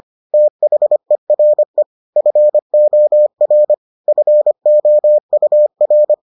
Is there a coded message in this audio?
Yes: Morse code from 0.3 to 6.1 s, heard as a keyed tone, about 25 wpm, 600 Hz. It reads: THERE FOR FOUR